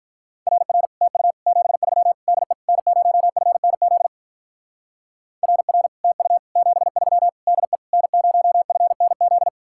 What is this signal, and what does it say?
Morse code, heard as a keyed tone, about 44 wpm, 700 Hz. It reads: RR TU 73 DE N0FNZ RR TU 73 DE N0FNZ